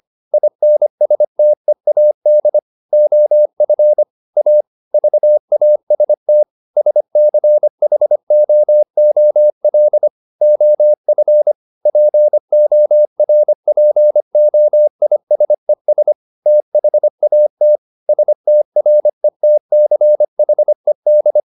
Morse code, heard as a keyed tone, about 25 wpm, 600 Hz. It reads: INSTEAD OF A VAST SCHOOL OF PORPOISES THAT STRETCHED